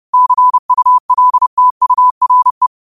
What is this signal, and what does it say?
Morse code, heard as a keyed tone, about 30 wpm, 1000 Hz. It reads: CULTURE
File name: qotdfiles / 30wpm